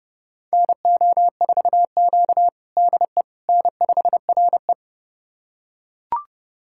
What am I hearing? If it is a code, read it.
Morse code, heard as a keyed tone, about 30 wpm, 700 Hz. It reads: NO4Q DE N5RE
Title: USA Callsign DE Callsign - 4 Characters - 30wpm